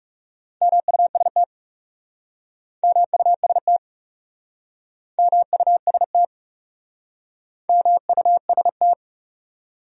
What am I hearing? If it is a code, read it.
Morse code, heard as a keyed tone, about 40 wpm, 700 Hz. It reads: MUST MUST MUST MUST